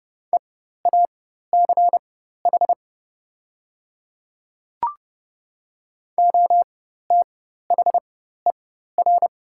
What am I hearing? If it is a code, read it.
Morse code, heard as a keyed tone, about 30 wpm, spaced as for 14 wpm, 700 Hz. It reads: EACH OTHER
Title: 4x Character Spacing - Top 100 Words - 30wpm